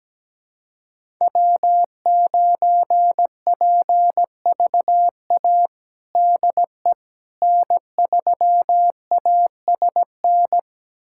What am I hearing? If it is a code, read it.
Morse code, heard as a keyed tone, about 17 wpm, 700 Hz. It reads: W9PVA DE N3ASN
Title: USA Callsign DE Callsign - 5 Characters - 17wpm